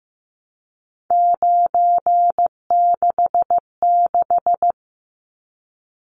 Morse code, heard as a keyed tone, about 15 wpm, 700 Hz. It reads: 966
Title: Numbers - 3-Digit 15wpm